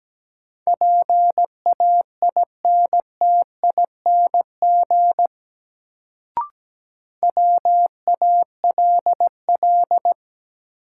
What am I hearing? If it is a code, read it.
Morse code, heard as a keyed tone, about 17 wpm, 700 Hz. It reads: PAINTING WALL
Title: Sentences from Top 1000 Words - Review Part 6 - New Word Drill - 17wpm